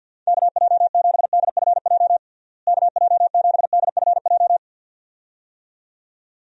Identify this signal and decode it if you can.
Morse code, heard as a keyed tone, about 49 wpm, 700 Hz. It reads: KJ7DFJ KJ7DFJ
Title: USA Call Signs - 6 Characters - Repeated 2x - 50wpm